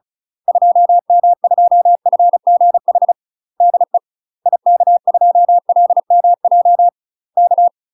Morse code, heard as a keyed tone, about 34 wpm, 700 Hz. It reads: 2M2FGH DE IK2LMJ K
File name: UK MASTER QSOs TRAINING / 35WPM / UK MASTER QSOs 0004